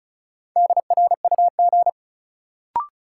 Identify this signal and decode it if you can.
Morse code, heard as a keyed tone, about 36 wpm, 700 Hz. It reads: DRUG
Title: Sets of 5 Words - New Word Drill - 35wpm